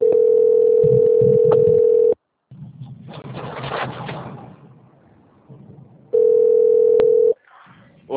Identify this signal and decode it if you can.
Ringback tone (North America), 440 + 480 Hz.